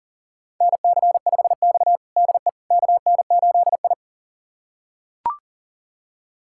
Morse code, heard as a keyed tone, about 40 wpm, 690 Hz. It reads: NC5X DE KN8I
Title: USA Callsign DE Callsign - 4 Characters - 40wpm